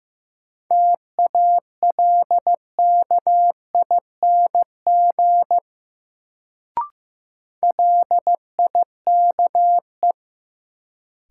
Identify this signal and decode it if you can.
Morse code, heard as a keyed tone, about 15 wpm, 700 Hz. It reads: TALKING LIKE
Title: Sets of 2 Words - New Word Drill - 15wpm